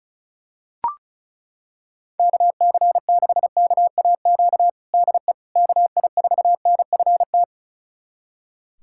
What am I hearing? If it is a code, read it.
Morse code, heard as a keyed tone, about 36 wpm, 700 Hz. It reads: KC6KAQ DE KI4NFT
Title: USA Callsign DE Callsign - 6 Characters - 35wpm